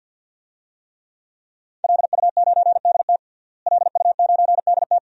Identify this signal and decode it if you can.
Morse code, heard as a keyed tone, about 50 wpm, 700 Hz. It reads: LU9DT LU9DT
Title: DXCC-100 Call Signs - 5 Characters - Repeated 2x - 50wpm